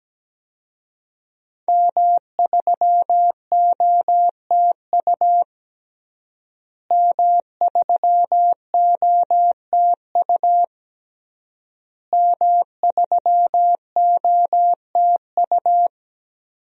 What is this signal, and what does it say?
Morse code, heard as a keyed tone, about 17 wpm, 700 Hz. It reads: M3OTU M3OTU M3OTU
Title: DXCC-100 Call Signs - 5 Characters - Repeated 3x - 17wpm